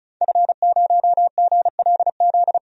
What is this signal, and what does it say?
Morse code, heard as a keyed tone, about 34 wpm, 700 Hz. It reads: F0GLZ
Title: DXCC-100 Call Signs - 5 Characters - 35wpm